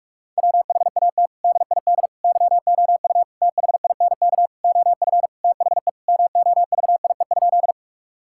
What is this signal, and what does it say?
Morse code, heard as a keyed tone, about 41 wpm, 700 Hz. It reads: WHAT DID YOU THINK OF THE MOVIE?
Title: Sentences from Top 1000 Words - Review Part 5 - 45wpm